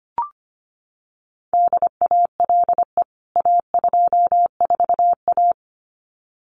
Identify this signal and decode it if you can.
Morse code, heard as a keyed tone, about 25 wpm, 700 Hz. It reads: DALE A24A
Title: CWT Exchange 25wpm